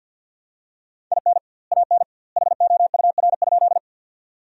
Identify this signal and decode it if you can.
Morse code, heard as a keyed tone, about 50 wpm, 700 Hz. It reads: IN AN HOUR?
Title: 2x Word-Spacing - Sentences from Top 400 Words - 50wpm